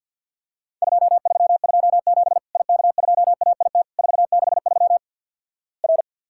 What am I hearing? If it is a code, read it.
Morse code, heard as a keyed tone, about 50 wpm, 700 Hz. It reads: 2327 IK2AIT 463 R
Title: Worked All Europe DX Contest - QTC Exchange - 50wpm